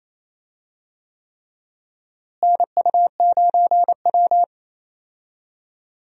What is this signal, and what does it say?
Morse code, heard as a keyed tone, about 27 wpm, 700 Hz. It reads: NU9W
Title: USA Call Signs - 4 Characters - Repeated 3x - 28wpm